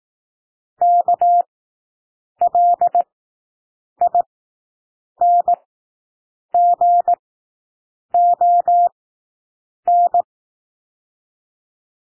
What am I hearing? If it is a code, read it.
Morse code, heard as a keyed tone, about 18 wpm, spaced as for 7 wpm, 700 Hz. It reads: KLINGON
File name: SOUND WORDS 18 wpm